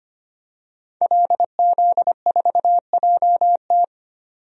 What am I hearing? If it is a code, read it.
Morse code, heard as a keyed tone, about 25 wpm, 700 Hz. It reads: LZ4JT
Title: DXCC-100 Call Signs - 5 Characters - 25wpm